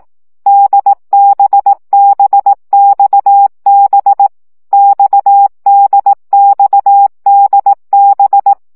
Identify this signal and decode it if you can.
Morse code, heard as a keyed tone, about 18 wpm, 800 Hz. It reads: DBBXB XDXDB
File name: CW gyakorlas d b x 15 baud